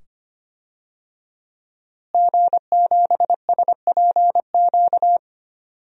Morse code, heard as a keyed tone, about 25 wpm, 700 Hz. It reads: G7SPQ
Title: DXCC-100 Call Signs - 5 Characters - Repeated 2x - 25wpm